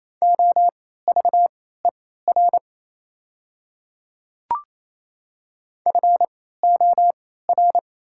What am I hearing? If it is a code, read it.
Morse code, heard as a keyed tone, about 28 wpm, spaced as for 16 wpm, 700 Hz. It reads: OVER FOR
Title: 3x Character Spacing - Top 100 Words - 28wpm